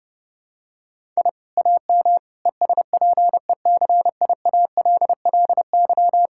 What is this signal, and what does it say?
Morse code, heard as a keyed tone, about 30 wpm, 700 Hz. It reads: I AM ESPECIALLY
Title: Sentences from Top 1000 Words - Review Part 6 - 30wpm